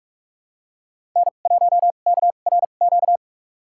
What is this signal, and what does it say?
Morse code, heard as a keyed tone, about 43 wpm, spaced as for 32 wpm, 700 Hz. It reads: N1KRQ
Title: USA Call Signs - 2x Character Spacing - 45wpm